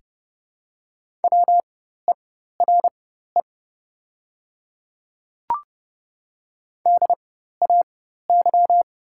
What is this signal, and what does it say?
Morse code, heard as a keyed tone, about 30 wpm, spaced as for 14 wpm, 700 Hz. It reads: WERE DAY
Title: 4x Character Spacing - Top 100 Words - 30wpm